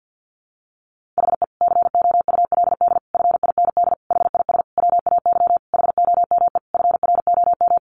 Morse code, heard as a keyed tone, about 50 wpm, 700 Hz. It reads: HE COULD FIND HIS WAY HOME FROM
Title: Sentences from Top 1400 Words 50wpm